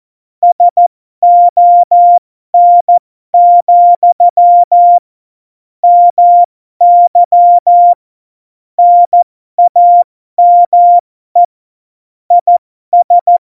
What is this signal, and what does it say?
Morse code, heard as a keyed tone, about 14 wpm, spaced as for 12 wpm, 700 Hz. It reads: SON, MY NAME IS
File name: Morsekurs / Witze